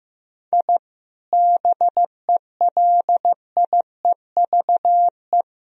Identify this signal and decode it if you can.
Morse code, heard as a keyed tone, about 15 wpm, 700 Hz. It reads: I BELIEVE
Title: Sentences from Top 1000 Words - Review Part 4 - 15wpm